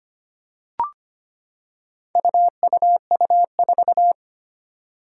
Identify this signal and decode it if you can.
Morse code, heard as a keyed tone, about 25 wpm, 700 Hz. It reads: UUU4
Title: Four Challenging Letter-Number Combinations 25wpm